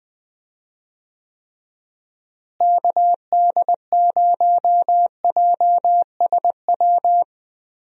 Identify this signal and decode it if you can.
Morse code, heard as a keyed tone, about 20 wpm, 700 Hz. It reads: KD0JSW
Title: USA Call Signs - 6 Characters - Repeated 3x - 20wpm